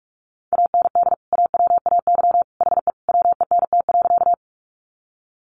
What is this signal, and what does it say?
Morse code, heard as a keyed tone, about 45 wpm, 700 Hz. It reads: AND AWAY HE WENT.
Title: Sentences from Top 600 Words 45wpm